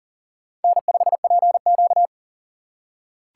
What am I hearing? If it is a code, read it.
Morse code, heard as a keyed tone, about 40 wpm, 700 Hz. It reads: N5PQ
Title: USA Call Signs - 4 Characters - 3x Character Spacing - 40wpm